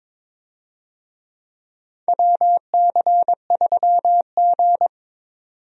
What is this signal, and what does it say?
Morse code, heard as a keyed tone, about 22 wpm, 700 Hz. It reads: WC3G
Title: USA Call Signs - 4 Characters - 22wpm